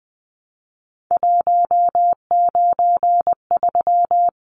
Morse code, heard as a keyed tone, about 20 wpm, 700 Hz. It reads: 193
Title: Numbers - 3-Digit 20wpm